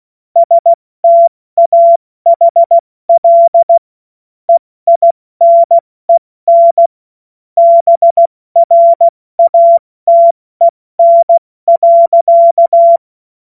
Morse code, heard as a keyed tone, about 16 wpm, 680 Hz. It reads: STAHL EINEN BRATEN.
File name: Morsekurs / Witze / Witz 04. Oktober 2022 70 F80